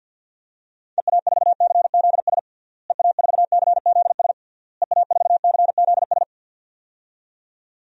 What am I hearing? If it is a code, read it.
Morse code, heard as a keyed tone, about 48 wpm, 700 Hz. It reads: EA4CZS EA4CZS EA4CZS
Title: DXCC-100 Call Signs - 6 Characters - Repeated 3x - 50wpm